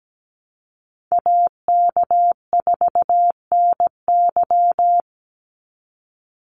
Morse code, heard as a keyed tone, about 17 wpm, 700 Hz. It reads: AK4NY
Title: Call Signs - 17wpm